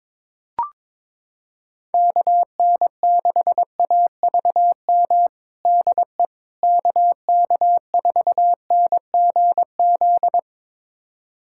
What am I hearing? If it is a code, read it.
Morse code, heard as a keyed tone, about 22 wpm, 700 Hz. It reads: KN6AVM DE KK4NGZ
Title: USA Callsign DE Callsign 22wpm